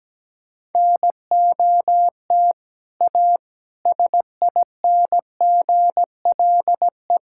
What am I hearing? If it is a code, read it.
Morse code, heard as a keyed tone, about 17 wpm, 700 Hz. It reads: NOT A SINGLE
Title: Sentences from Top 1000 Words - Review Part 5 - 17wpm